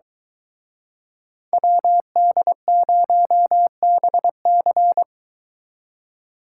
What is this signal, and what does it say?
Morse code, heard as a keyed tone, about 23 wpm, 700 Hz. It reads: WD0BC